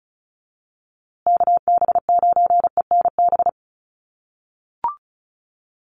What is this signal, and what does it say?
Morse code, heard as a keyed tone, about 34 wpm, 700 Hz. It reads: KB9ENB
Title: USA Call Signs Repeated 3x 35wpm